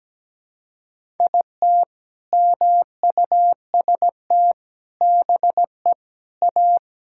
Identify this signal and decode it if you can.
Morse code, heard as a keyed tone, about 17 wpm, 700 Hz. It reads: IT MUST BE A
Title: Sentences from Top 1000 Words - Review Part 6 - 17wpm